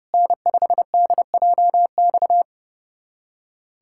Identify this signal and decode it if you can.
Morse code, heard as a keyed tone, about 30 wpm, 700 Hz. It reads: N5DJX
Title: USA Callsign DE Callsign 30wpm